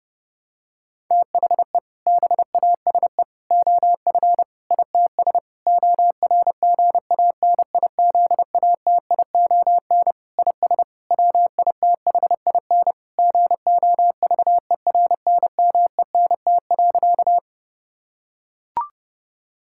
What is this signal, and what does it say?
Morse code, heard as a keyed tone, about 30 wpm, 700 Hz. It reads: THE BASE OF ITS ORGANIZATION IS WITHIN GOVERNMENT.
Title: Sentences from Top 1000 Words - Review Part 5 - 30wpm